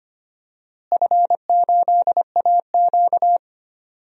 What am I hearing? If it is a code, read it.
Morse code, heard as a keyed tone, about 25 wpm, 700 Hz. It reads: F8AQ
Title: DXCC-100 Call Signs - 4 Characters - Repeated 2x - 25wpm